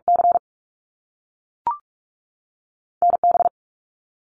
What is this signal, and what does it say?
Morse code, heard as a keyed tone, about 46 wpm, 700 Hz. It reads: C NB
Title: Canadian Provinces and Territories 45wpm